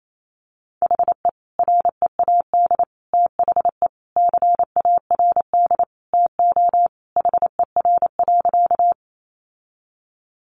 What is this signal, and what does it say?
Morse code, heard as a keyed tone, about 28 wpm, 700 Hz. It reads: HE READ THE CARD TO HER.
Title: Sentences from Top 600 Words 28wpm